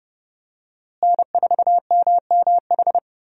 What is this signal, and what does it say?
Morse code, heard as a keyed tone, about 29 wpm, 700 Hz. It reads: N4MMH
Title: USA Call Signs - 5 Characters - Repeated 3x - 30wpm